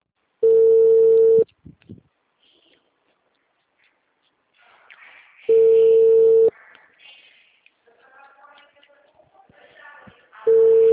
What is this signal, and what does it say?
Ringback tone (China, 450 Hz), with speech.
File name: out- -2048- - - .28231